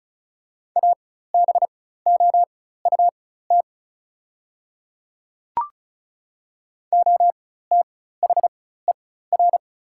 Morse code, heard as a keyed tone, about 34 wpm, spaced as for 16 wpm, 690 Hz. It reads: ABOUT OTHER
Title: 4x Character Spacing - Top 100 Words - 35wpm